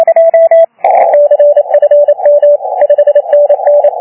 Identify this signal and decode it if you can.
Morse code, heard as a keyed tone, about 27 wpm, 580 Hz. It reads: CFM 5NN
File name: CUC351 PILEUP OK1DCS